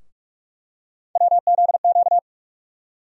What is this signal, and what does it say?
Morse code, heard as a keyed tone, about 46 wpm, 700 Hz. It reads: WZQ
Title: USA Call Sign Postfixes - 3 Letters 45wpm